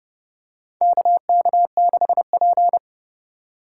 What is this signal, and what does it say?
Morse code, heard as a keyed tone, about 32 wpm, 700 Hz. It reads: KK6P
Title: USA Call Signs - 4 Characters - 3x Character Spacing - 30wpm